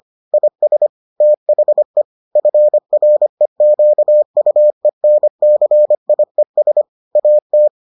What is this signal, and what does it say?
Morse code, heard as a keyed tone, about 25 wpm, 600 Hz. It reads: IS THE FREQUENCIES AT